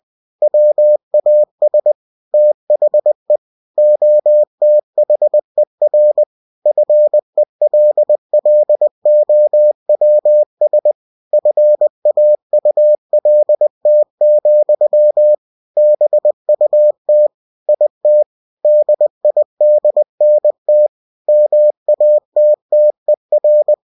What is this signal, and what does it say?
Morse code, heard as a keyed tone, about 20 wpm, 600 Hz. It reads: WAS THE OTHER FELLOWS FAULT, BUT IT DIDNT MATTER
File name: cw-books / winnie-the-pooh / winnie-20wpm-page-0102